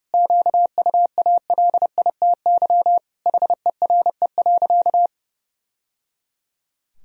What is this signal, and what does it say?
Morse code, heard as a keyed tone, about 28 wpm, 700 Hz. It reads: QUALITY HERE.
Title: Sentences from Top 1000 Words - Review Part 4 - 30wpm